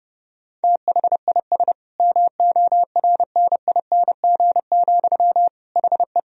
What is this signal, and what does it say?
Morse code, heard as a keyed tone, about 30 wpm, 700 Hz. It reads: THIS MORNING, HE
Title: Sentences from Top 1000 Words - Review Part 1 - 30wpm